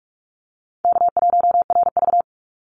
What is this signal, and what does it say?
Morse code, heard as a keyed tone, about 44 wpm, 700 Hz. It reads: K1RV
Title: Easy Call Signs Repeated 2x - 45wpm